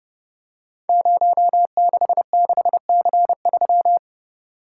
Morse code, heard as a keyed tone, about 30 wpm, 700 Hz. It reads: 066C3
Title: Code Groups - 5 Letter:Numbers - 30wpm